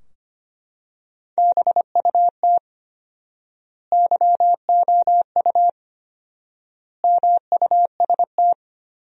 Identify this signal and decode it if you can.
Morse code, heard as a keyed tone, about 25 wpm, 700 Hz. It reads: BUT YOU MUST
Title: 4x Word-Spacing - Sentences from Top 300 Words - 25wpm